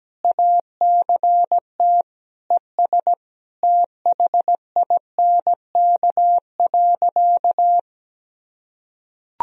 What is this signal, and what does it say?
Morse code, heard as a keyed tone, about 17 wpm, 700 Hz. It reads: ACT ES THINK.
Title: Sentences from Top 500 Words and Abbreviations 17wpm